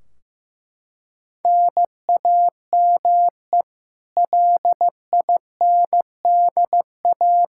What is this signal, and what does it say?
Morse code, heard as a keyed tone, about 15 wpm, 700 Hz. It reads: NAME LINDA
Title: Practice QSO - Exchange 1 - Beginning - 15wpm